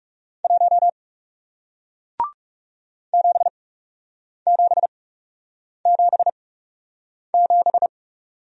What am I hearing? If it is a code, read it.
Morse code, heard as a keyed tone, about 40 wpm, spaced as for 8 wpm, 700 Hz. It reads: 1 7777